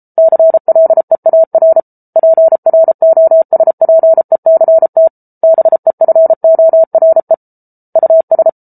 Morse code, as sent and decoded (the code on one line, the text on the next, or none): -.-. .-.. .- .-. / .--. .-. --- ... .--. -.-. - / -... ..-. --- .-. / ..- ...
CLEAR PROSPECT BEFORE US